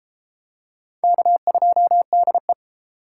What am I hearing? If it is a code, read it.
Morse code, heard as a keyed tone, about 30 wpm, 700 Hz. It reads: K2DE